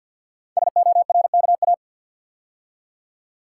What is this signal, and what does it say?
Morse code, heard as a keyed tone, about 47 wpm, 700 Hz. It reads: SORKA
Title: Code Groups - 5 Letters - 50wpm